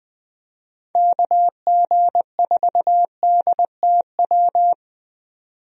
Morse code, heard as a keyed tone, about 20 wpm, 700 Hz. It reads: KG4DTW
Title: USA Call Signs - 6 Characters - 20wpm